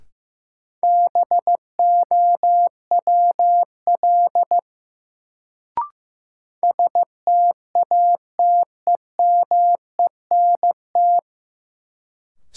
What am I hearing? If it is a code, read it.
Morse code, heard as a keyed tone, about 15 wpm, 700 Hz. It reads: BOWL STATEMENT
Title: Sentences from Top 1200 Words - New Word Drill - 15wpm